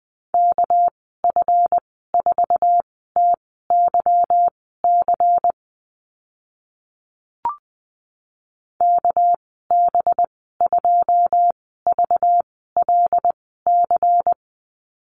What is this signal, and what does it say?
Morse code, heard as a keyed tone, about 20 wpm, spaced as for 14 wpm, 700 Hz. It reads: KF4TYC KB2VLC
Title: 245 Lesson 39g - Letter z - Call Signs 20wpm